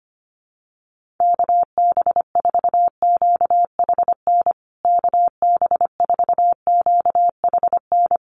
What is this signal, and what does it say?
Morse code, heard as a keyed tone, about 25 wpm, 700 Hz. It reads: KB4QHN KB4QHN
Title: USA Call Signs Repeated 3x 25wpm